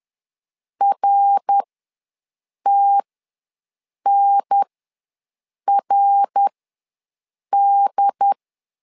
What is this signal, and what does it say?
Morse code, heard as a keyed tone, about 10 wpm, spaced as for 6 wpm, 800 Hz. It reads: RTNRD
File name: LEZIONI / 3 MWRO / 3 MWR ws 10wpm 2151